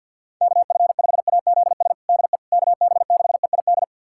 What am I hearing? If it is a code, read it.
Morse code, heard as a keyed tone, about 52 wpm, 700 Hz. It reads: KF5AZS DE KB6EID
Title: USA Callsign DE Callsign - 6 Characters - 50wpm